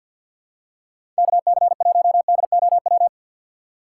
Morse code, heard as a keyed tone, about 52 wpm, 700 Hz. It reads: KC1DOW
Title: USA Call Signs - 6 Characters - 8x Character Spacing - 50wpm